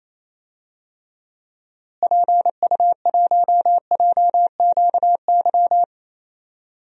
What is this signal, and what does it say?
Morse code, heard as a keyed tone, about 28 wpm, 700 Hz. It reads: PU1JQY
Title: DXCC-100 Call Signs - 6 Characters - Repeated 3x - 28wpm